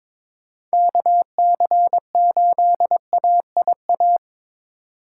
Morse code, heard as a keyed tone, about 22 wpm, 700 Hz. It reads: KC8AIA
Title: USA Call Signs - 3x Character Spacing - 22wpm